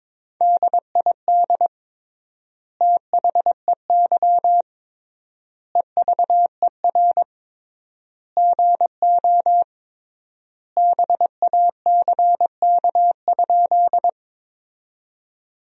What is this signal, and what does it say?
Morse code, heard as a keyed tone, about 21 wpm, 700 Hz. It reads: DID THEY EVER GO BACK?
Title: 3x Word-Spacing - Sentences from Top 400 Words - 22wpm